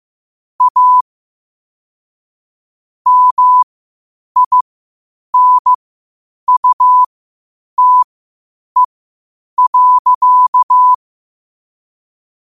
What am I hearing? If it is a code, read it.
Morse code, heard as a keyed tone, about 15 wpm, spaced as for 8 wpm, 1000 Hz. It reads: A MINUTE.
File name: qotdfiles / 8wpm